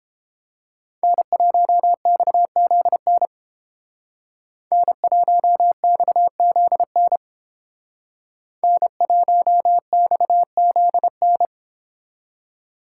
Morse code, heard as a keyed tone, about 31 wpm, 700 Hz. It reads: N1XZN N1XZN N1XZN